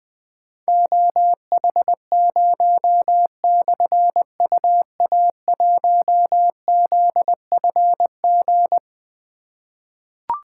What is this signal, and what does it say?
Morse code, heard as a keyed tone, about 21 wpm, 700 Hz. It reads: OH0/UA1ZFG
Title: Difficult Call Signs 20wpm